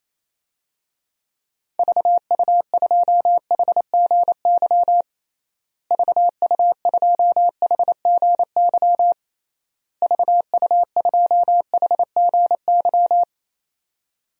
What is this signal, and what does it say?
Morse code, heard as a keyed tone, about 28 wpm, 700 Hz. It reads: VU2HGY VU2HGY VU2HGY
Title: DXCC-100 Call Signs - 6 Characters - Repeated 3x - 28wpm